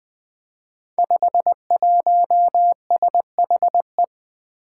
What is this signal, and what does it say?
Morse code, heard as a keyed tone, about 20 wpm, 700 Hz. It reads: 51SHE
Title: Five Challenging Letter-Number Combinations 20wpm